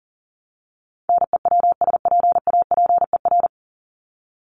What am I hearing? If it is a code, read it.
Morse code, heard as a keyed tone, about 40 wpm, 700 Hz. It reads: NEWSPAPER
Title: Top 751-1000 Words 40wpm